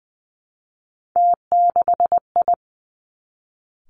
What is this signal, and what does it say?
Morse code, heard as a keyed tone, about 20 wpm, 700 Hz. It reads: T6I
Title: Three Challenging Letter-Number Combinations 20wpm